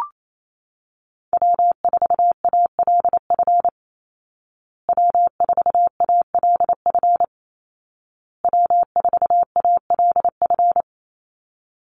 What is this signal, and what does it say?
Morse code, heard as a keyed tone, about 28 wpm, 700 Hz. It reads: W4ALF W4ALF W4ALF
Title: Call Signs Repeated 2x - 28wpm